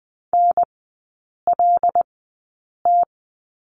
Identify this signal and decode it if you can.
Morse code, heard as a keyed tone, about 20 wpm, spaced as for 8 wpm, 700 Hz. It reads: NLT
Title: 074 Lesson 13e - Letter L - Three Characters 20wpm